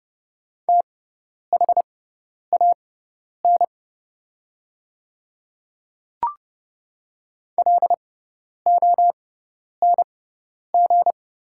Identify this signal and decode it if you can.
Morse code, heard as a keyed tone, about 30 wpm, spaced as for 10 wpm, 700 Hz. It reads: THAN LONG
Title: 6x Character Spacing - Top 100 Words - 30wpm